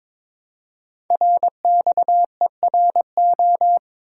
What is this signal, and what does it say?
Morse code, heard as a keyed tone, about 22 wpm, 700 Hz. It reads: RXERO
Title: Code Groups - 5 Letters - 22wpm